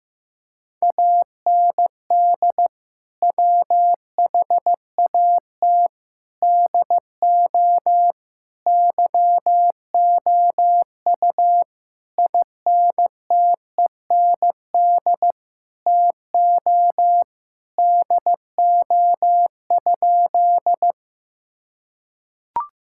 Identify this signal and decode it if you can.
Morse code, heard as a keyed tone, about 15 wpm, 700 Hz. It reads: AND WHAT DO YOU INTEND TO DO?
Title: Sentences from Top 1600 Words 15wpm